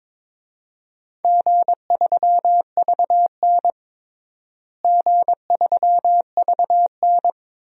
Morse code, heard as a keyed tone, about 22 wpm, 700 Hz. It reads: G3VN G3VN
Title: DXCC-100 Call Signs - 4 Characters - Repeated 2x - 22wpm